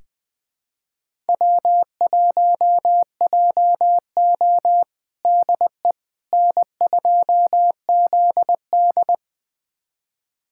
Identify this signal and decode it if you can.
Morse code, heard as a keyed tone, about 20 wpm, 700 Hz. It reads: W1JO DE N2ZD
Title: USA Callsign DE Callsign - 4 Characters - 20wpm